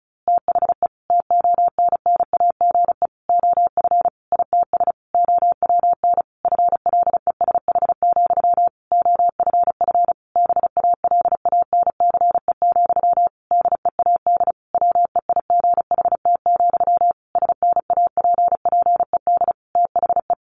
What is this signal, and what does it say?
Morse code, heard as a keyed tone, about 34 wpm, 700 Hz. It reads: THE TONNAGE OF ITS OWN FLESH, OFF BALANCE, DEAD WEIGHT, SNAPPED THE
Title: A Sound Of Thunder 35wpm